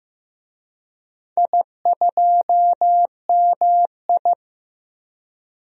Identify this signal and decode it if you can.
Morse code, heard as a keyed tone, about 15 wpm, 700 Hz. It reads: I2MI
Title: DXCC-100 Call Signs - 4 Characters - Repeated 2x - 15wpm